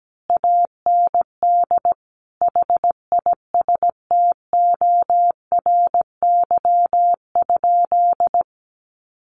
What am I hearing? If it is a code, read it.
Morse code, heard as a keyed tone, about 17 wpm, 700 Hz. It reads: AND HISTORY?
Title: Sentences from Top 600 Words 17wpm